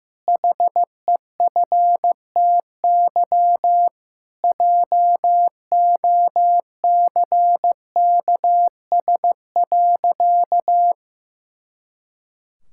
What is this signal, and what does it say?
Morse code, heard as a keyed tone, about 15 wpm, 700 Hz. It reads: HEFTY JOCKS.
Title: Pangram Challenge v2 - 15wpm